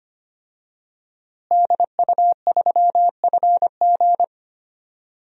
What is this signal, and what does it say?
Morse code, heard as a keyed tone, about 25 wpm, 700 Hz. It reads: DU3FG
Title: DXCC-100 Call Signs - 5 Characters - Repeated 2x - 25wpm